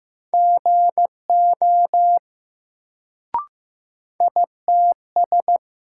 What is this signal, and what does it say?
Morse code, heard as a keyed tone, about 15 wpm, 700 Hz. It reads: GO ITS
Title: Sets of 2 Words - New Word Drill - 15wpm